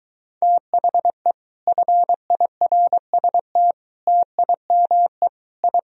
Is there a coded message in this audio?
Yes: Morse code from 0.4 to 5.8 s, heard as a keyed tone, about 23 wpm, 700 Hz. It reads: THE FIRST TIME I